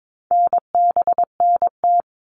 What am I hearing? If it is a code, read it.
Morse code, heard as a keyed tone, about 22 wpm, 700 Hz. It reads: NBNT
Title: All Cut Numbers - 4-Digit 22wpm